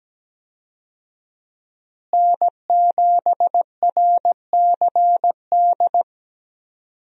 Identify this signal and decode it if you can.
Morse code, heard as a keyed tone, about 17 wpm, 700 Hz. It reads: N7RCD
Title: USA Call Signs - 5 Characters - Repeated 3x - 17wpm